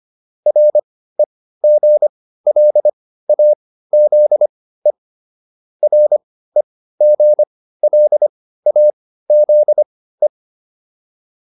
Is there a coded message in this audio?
Yes: Morse code from 0.5 to 10.3 s, heard as a keyed tone, about 25 wpm, spaced as for 15 wpm, 600 Hz. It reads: REGLAZE REGLAZE